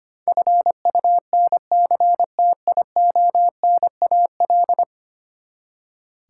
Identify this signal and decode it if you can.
Morse code, heard as a keyed tone, about 25 wpm, 700 Hz. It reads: FUNCTIONAL
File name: Ten Letter Words - 25wpm